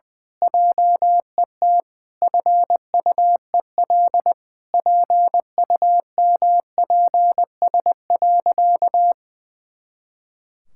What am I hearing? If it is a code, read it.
Morse code, heard as a keyed tone, about 21 wpm, 700 Hz. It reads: JET FUEL PUMPS.
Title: Pangram Challenge v2 - 20wpm